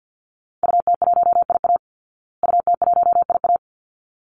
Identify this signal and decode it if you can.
Morse code, heard as a keyed tone, about 50 wpm, 700 Hz. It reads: UT1IA UT1IA
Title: Call Signs Repeated 3x - 50wpm